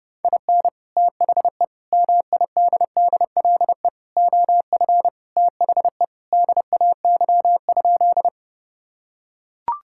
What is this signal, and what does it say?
Morse code, heard as a keyed tone, about 32 wpm, 700 Hz. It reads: IN THE MIDDLE OF THE DAY?
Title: Sentences from Top 1000 Words - Review Part 6 - 30wpm